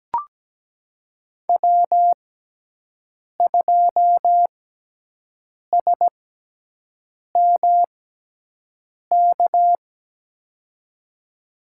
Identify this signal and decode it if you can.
Morse code, heard as a keyed tone, about 17 wpm, spaced as for 6 wpm, 700 Hz. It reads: W2SMK
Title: USA Call Signs - 5 Characters - 6x Character Spacing - 17wpm